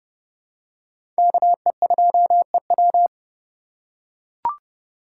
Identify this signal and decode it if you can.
Morse code, heard as a keyed tone, about 30 wpm, 700 Hz. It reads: KE2EW
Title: USA Call Signs - 2x Character Spacing - 30wpm